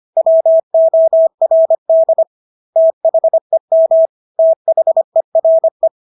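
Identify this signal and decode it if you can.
Morse code, heard as a keyed tone, about 25 wpm, 650 Hz. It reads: WORD THEM THERE
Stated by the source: Morse practice of English words at 25 wpm from CW med Gnister.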